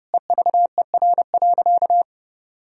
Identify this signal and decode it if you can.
Morse code, heard as a keyed tone, about 29 wpm, 700 Hz. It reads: EVER.
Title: Sentences from Top 1000 Words - Review Part 4 - 30wpm